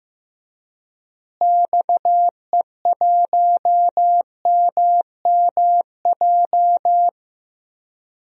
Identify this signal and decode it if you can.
Morse code, heard as a keyed tone, about 15 wpm, 700 Hz. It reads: XE1MMJ
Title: DXCC-100 Call Signs - 6 Characters - Repeated 2x - 15wpm